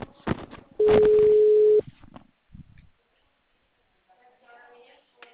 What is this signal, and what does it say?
Ringback tone (Europe, 425 Hz), with speech.
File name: out- -2009- - - .51264